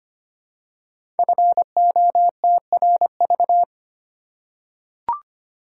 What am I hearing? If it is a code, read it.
Morse code, heard as a keyed tone, about 25 wpm, 700 Hz. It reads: FOTRV
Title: Code Groups - 5 Letters - 25wpm